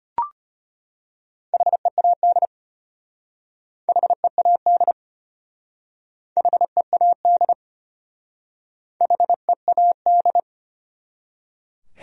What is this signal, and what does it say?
Morse code, heard as a keyed tone, about 32 wpm, 700 Hz. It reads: HEAD HEAD HEAD HEAD